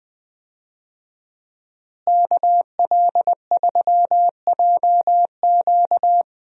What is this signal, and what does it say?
Morse code, heard as a keyed tone, about 20 wpm, 700 Hz. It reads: KL3JQ
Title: USA Call Signs - 5 Characters - Repeated 2x - 20wpm